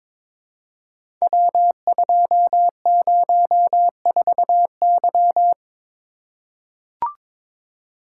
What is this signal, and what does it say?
Morse code, heard as a keyed tone, about 23 wpm, 700 Hz. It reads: W204Y
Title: Code Groups - 5 Letter:Numbers - 22wpm